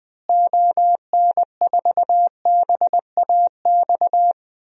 Morse code, heard as a keyed tone, about 21 wpm, 700 Hz. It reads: ON4BAX
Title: DXCC-100 Call Signs - 6 Characters - Repeated 2x - 20wpm